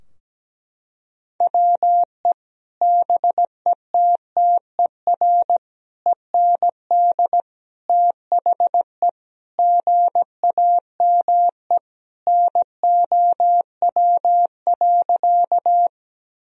Morse code, heard as a keyed tone, about 17 wpm, 700 Hz. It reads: WE BETTER END THE GAME NOW.
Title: Sentences from Top 1000 Words - Review Part 3 - 17wpm